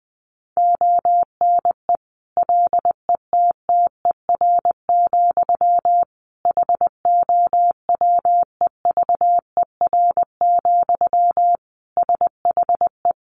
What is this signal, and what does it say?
Morse code, heard as a keyed tone, about 20 wpm, 700 Hz. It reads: ONE LETTER, HOWEVER, SHE
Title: Sentences from Top 1200 Words 20wpm